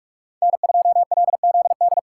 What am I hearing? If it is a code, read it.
Morse code, heard as a keyed tone, about 43 wpm, 700 Hz. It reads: N2LZD